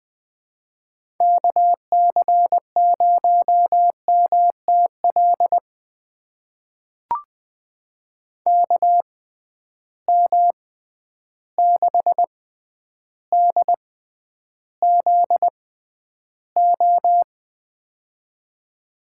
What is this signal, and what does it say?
Morse code, heard as a keyed tone, about 20 wpm, spaced as for 10 wpm, 700 Hz. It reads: KC0MTL KM6DZO
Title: USA Call Signs - 6 Characters - 6x Character Spacing - 20wpm